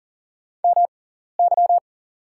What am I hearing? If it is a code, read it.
Morse code, heard as a keyed tone, about 40 wpm, spaced as for 14 wpm, 700 Hz. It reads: MY
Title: 6x Character Spacing - Top 100 Words - 40wpm